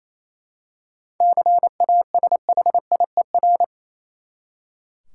Morse code, heard as a keyed tone, about 28 wpm, 700 Hz. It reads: CASHIER
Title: Practice QSO - Exchange 3 - More Info - 28wpm